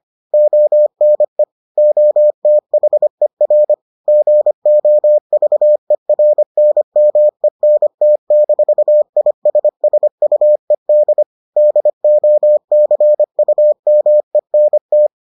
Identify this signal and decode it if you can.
Morse code, heard as a keyed tone, about 25 wpm, 600 Hz. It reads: ONE OTHER GOVERNMENT-ISSUED DOCUMENT